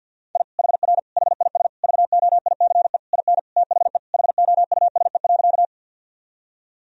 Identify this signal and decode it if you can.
Morse code, heard as a keyed tone, about 48 wpm, 700 Hz. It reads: I HR HIS VOICE IN THE HOUSE.
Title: Sentences from Top 500 Words and Abbreviations 50wpm